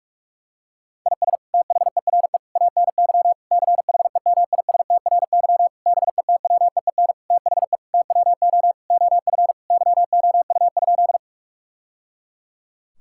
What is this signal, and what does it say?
Morse code, heard as a keyed tone, about 46 wpm, 700 Hz. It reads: IS THERE ANY CHEMISTRY BETWEEN THE TWO OF YOU?
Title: Sentences from Top 1000 Words - Review Part 6 - 45wpm